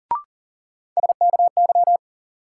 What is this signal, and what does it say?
Morse code, heard as a keyed tone, about 40 wpm, 700 Hz. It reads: SKY
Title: Sentences from Top 1200 Words - New Word Drill Alt - 40wpm